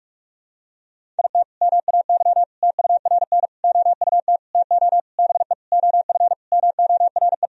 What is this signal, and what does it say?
Morse code, heard as a keyed tone, about 46 wpm, 700 Hz. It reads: IT MAY TURN OUT TO BE OF MORE
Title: Sentences from Top 400 Words and Abbreviations 45wpm